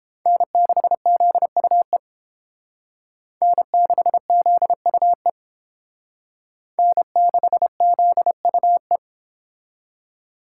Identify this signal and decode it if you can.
Morse code, heard as a keyed tone, about 28 wpm, 700 Hz. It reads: N6ZUE N6ZUE N6ZUE